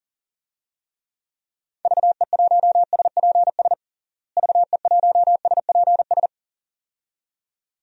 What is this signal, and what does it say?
Morse code, heard as a keyed tone, about 40 wpm, 700 Hz. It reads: VE1SPS VE1SPS
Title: DXCC-100 Call Signs - 6 Characters - Repeated 2x - 40wpm